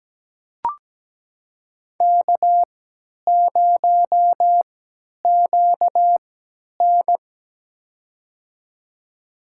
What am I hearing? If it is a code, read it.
Morse code, heard as a keyed tone, about 17 wpm, spaced as for 10 wpm, 700 Hz. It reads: K0QN
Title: USA Call Signs - 4 Characters - 3x Character Spacing - 17wpm